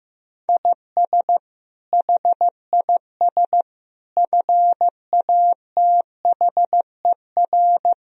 Morse code, heard as a keyed tone, about 16 wpm, 700 Hz. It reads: IS HIS FATHER
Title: Sentences from Top 1000 Words - Review Part 4 - 15wpm